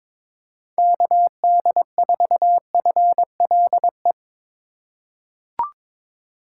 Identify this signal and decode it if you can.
Morse code, heard as a keyed tone, about 22 wpm, 700 Hz. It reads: KD4FLE
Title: USA Call Signs - 6 Characters - 3x Character Spacing - 22wpm